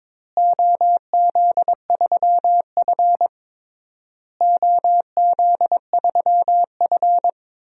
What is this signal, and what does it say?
Morse code, heard as a keyed tone, about 22 wpm, 700 Hz. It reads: OZ3F OZ3F
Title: DXCC-100 Call Signs - 4 Characters - Repeated 3x - 22wpm